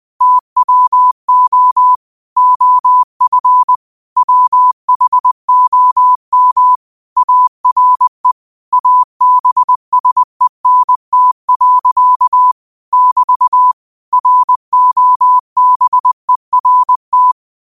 Morse code, heard as a keyed tone, about 20 wpm, 1000 Hz. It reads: TWO OF WHOM ARE ABSENT. <BT> ROBERT